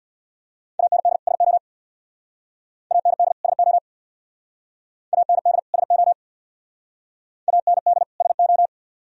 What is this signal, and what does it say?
Morse code, heard as a keyed tone, about 63 wpm, 700 Hz. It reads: AND SO AND SO AND SO AND SO